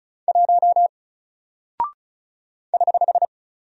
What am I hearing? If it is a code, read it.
Morse code, heard as a keyed tone, about 34 wpm, 700 Hz. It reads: 1 <HH>
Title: Single Character (Letter:Number:Punctuation:Prosign) Alt 35wpm